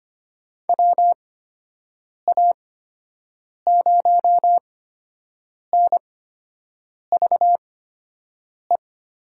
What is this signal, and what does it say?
Morse code, heard as a keyed tone, about 25 wpm, spaced as for 7 wpm, 700 Hz. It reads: WA0NVE
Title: USA Call Signs - 6 Characters - 8x Character Spacing - 25wpm